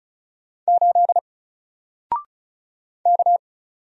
Morse code, heard as a keyed tone, about 36 wpm, 700 Hz. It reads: 8 K
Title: Single Letter-Number Alt 35wpm